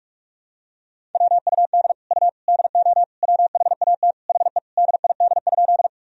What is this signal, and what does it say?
Morse code, heard as a keyed tone, about 46 wpm, 700 Hz. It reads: WUD U DO WHAT HE DID?
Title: Sentences from Top 100 Words and Abbreviations 45wpm